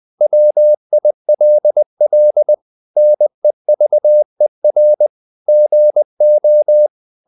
Morse code, heard as a keyed tone, about 20 wpm, 600 Hz. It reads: WILL NEVER GO
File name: the old man and the sea part 2 20wpm